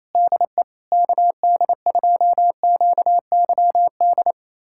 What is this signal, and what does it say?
Morse code, heard as a keyed tone, about 28 wpm, 700 Hz. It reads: DE KD2QYD
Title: USA Callsign DE Callsign - 6 Characters - 28wpm